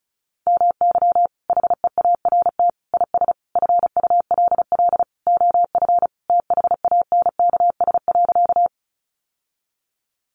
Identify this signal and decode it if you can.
Morse code, heard as a keyed tone, about 34 wpm, 700 Hz. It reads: MY HEART IS FULL OF THANKS.
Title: Sentences from Top 1200 Words 35wpm